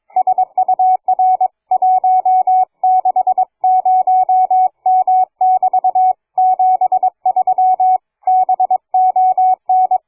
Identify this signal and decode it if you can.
Morse code, heard as a keyed tone, about 22 wpm, 740 Hz. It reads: SUR 160M<BT> 73 BON
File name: cw forum / fg5fr 160m 02h30